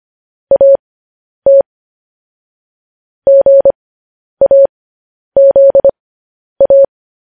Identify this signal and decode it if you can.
Morse code, heard as a keyed tone, about 25 wpm, spaced as for 10 wpm, 550 Hz. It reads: AT GAZA
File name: podcast / News headlines-1x-25x10